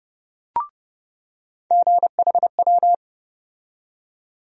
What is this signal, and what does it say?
Morse code, heard as a keyed tone, about 30 wpm, 700 Hz. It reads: GHW
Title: USA Call Sign Postfixes - 3 Letters 30wpm